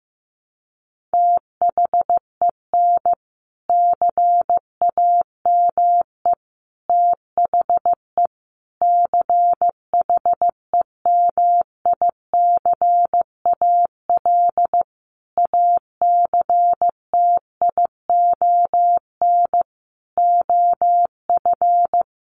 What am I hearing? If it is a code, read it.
Morse code, heard as a keyed tone, about 15 wpm, 700 Hz. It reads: THEN CAME THE CHEMICAL ACTION OF
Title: Sentences from Top 1200 Words 15wpm